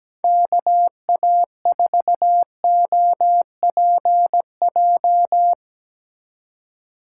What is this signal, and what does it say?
Morse code, heard as a keyed tone, about 17 wpm, 700 Hz. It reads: KA4OPJ
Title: USA Call Signs - 6 Characters - 4x Character Spacing - 17wpm